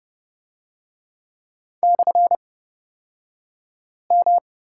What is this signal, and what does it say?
Morse code, heard as a keyed tone, about 30 wpm, spaced as for 5 wpm, 700 Hz. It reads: /M
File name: Lesson 37d - Letter x - Two Characters 30wpm